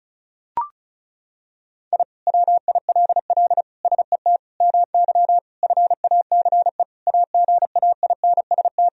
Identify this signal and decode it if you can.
Morse code, heard as a keyed tone, about 34 wpm, 700 Hz. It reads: I WILL SET MY FACE AGAINST
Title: Sentences from Top 1000 Words - Review Part 2 - 35wpm